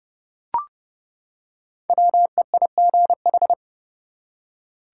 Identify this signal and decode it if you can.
Morse code, heard as a keyed tone, about 31 wpm, 700 Hz. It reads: WEIGH
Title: Five Letter Words - 30wpm